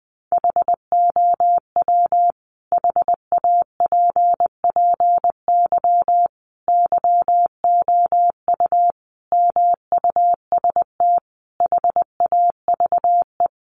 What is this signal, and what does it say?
Morse code, heard as a keyed tone, about 20 wpm, 700 Hz. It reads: HOW HAPPY YOU MUST HAVE
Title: Sentences from Top 1000 Words 20wpm